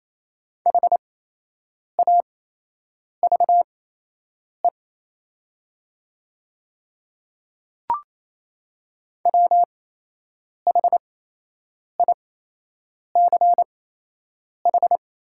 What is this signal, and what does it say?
Morse code, heard as a keyed tone, about 28 wpm, spaced as for 8 wpm, 700 Hz. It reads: HAVE WHICH
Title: 8x Character Spacing - Top 100 Words - 28wpm